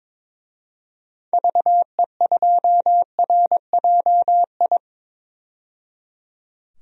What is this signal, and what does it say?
Morse code, heard as a keyed tone, about 22 wpm, 700 Hz. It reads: VE2RJI
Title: DXCC-100 Call Signs - 6 Characters - 22wpm